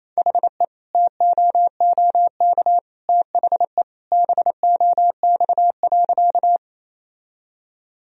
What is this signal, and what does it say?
Morse code, heard as a keyed tone, about 28 wpm, 700 Hz. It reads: HE TOOK THE BOX.
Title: Sentences from Top 1000 Words - Review Part 2 - 28wpm